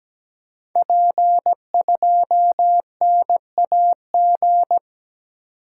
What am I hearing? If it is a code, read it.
Morse code, heard as a keyed tone, about 17 wpm, 700 Hz. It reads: P2NAG
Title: Code Groups - 5 Letter:Numbers - 17wpm